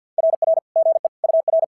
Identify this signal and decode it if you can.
Morse code, heard as a keyed tone, about 50 wpm, 650 Hz. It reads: RR GE UR
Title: SOTA QSO Practice 50wpm